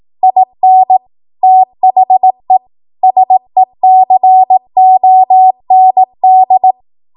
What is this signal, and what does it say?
Morse code, heard as a keyed tone, about 18 wpm, 750 Hz. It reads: IN THE SECOND
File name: Archive / 18 WPM / 18WPM